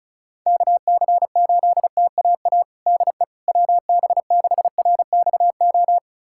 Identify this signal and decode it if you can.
Morse code, heard as a keyed tone, about 36 wpm, 700 Hz. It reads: KC8TAA DE WB6RXO
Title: USA Callsign DE Callsign - 6 Characters - 35wpm